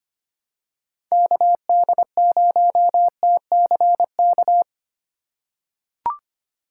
Morse code, heard as a keyed tone, about 25 wpm, 700 Hz. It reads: KD0TCK
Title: USA Call Signs - 3x Character Spacing - 25wpm